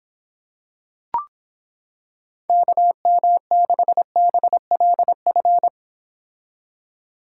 Morse code, heard as a keyed tone, about 26 wpm, 700 Hz. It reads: KM6BLF